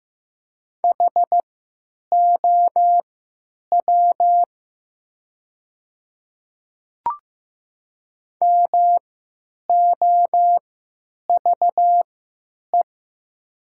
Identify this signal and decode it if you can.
Morse code, heard as a keyed tone, about 15 wpm, spaced as for 9 wpm, 700 Hz. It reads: HOW MOVE
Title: 3x Character Spacing - Top 300 Words - 15wpm